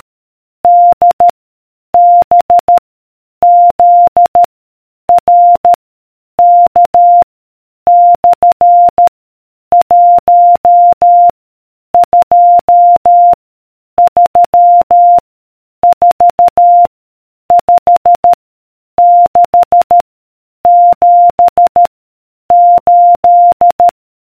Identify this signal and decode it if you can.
Morse code, heard as a keyed tone, about 13 wpm, 700 Hz. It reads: D B Z R K / 1 2 3 4 5 6 7 8